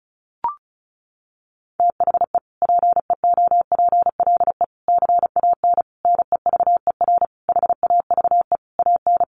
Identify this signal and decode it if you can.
Morse code, heard as a keyed tone, about 34 wpm, 700 Hz. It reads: THE PEOPLE CAN NEVER HAVE AN
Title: Sentences from Top 1400 Words 35wpm